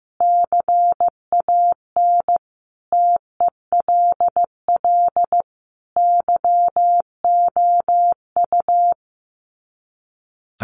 Morse code, heard as a keyed tone, about 15 wpm, 700 Hz. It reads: CAN TELL YOU
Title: Sets of 4 Words 15wpm